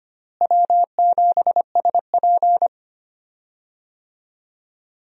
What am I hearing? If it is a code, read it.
Morse code, heard as a keyed tone, about 25 wpm, 700 Hz. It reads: W7SP
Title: USA Call Signs - 4 Characters - Repeated 3x - 25wpm